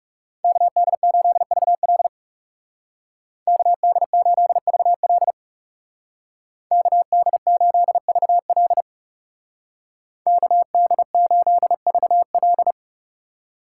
Morse code, heard as a keyed tone, about 35 wpm, 700 Hz. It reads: KD8VL KD8VL KD8VL KD8VL
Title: Speed Racing - USA 2x2 Call Signs 30wpm